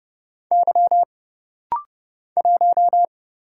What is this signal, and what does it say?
Morse code, heard as a keyed tone, about 30 wpm, 700 Hz. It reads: Y 1
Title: Single Letter-Number Alt - Rapid-Fire 30wpm